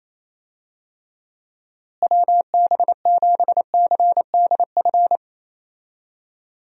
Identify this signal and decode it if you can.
Morse code, heard as a keyed tone, about 28 wpm, 700 Hz. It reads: WB7CDF
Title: USA Call Signs - 6 Characters - Repeated 3x - 28wpm